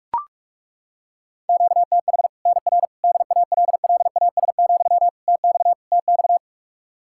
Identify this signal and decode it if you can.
Morse code, heard as a keyed tone, about 41 wpm, 700 Hz. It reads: QTH NR DALLAS, TX TX
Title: Practice QSO - Exchange 1 - Beginning - 45wpm